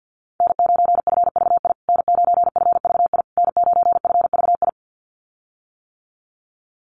Morse code, heard as a keyed tone, about 50 wpm, 700 Hz. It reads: N9FVI N9FVI N9FVI
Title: USA Call Signs Repeated 3x 50wpm